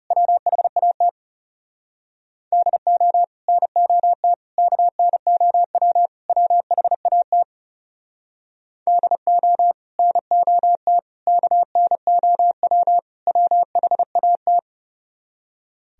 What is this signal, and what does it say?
Morse code, heard as a keyed tone, about 37 wpm, 700 Hz. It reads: WHAT DO NOT KNOW WHAT DO NOT KNOW WHAT
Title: Speed-Racing - Sets of 4 Words 30wpm